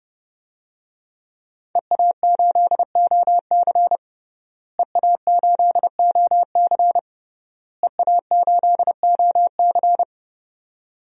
Morse code, heard as a keyed tone, about 30 wpm, 700 Hz. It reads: EA8OC EA8OC EA8OC
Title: DXCC-100 Call Signs - 5 Characters - Repeated 3x - 30wpm